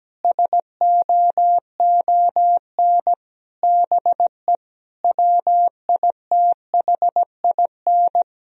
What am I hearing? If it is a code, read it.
Morse code, heard as a keyed tone, about 17 wpm, 700 Hz. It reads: SOON BE WITHIN
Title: Sentences from Top 1000 Words - Review Part 4 - 17wpm